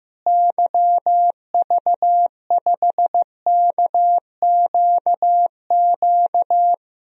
Morse code, heard as a keyed tone, about 15 wpm, 700 Hz. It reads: YV5KQQ
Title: DXCC-100 Call Signs - 6 Characters - Repeated 3x - 15wpm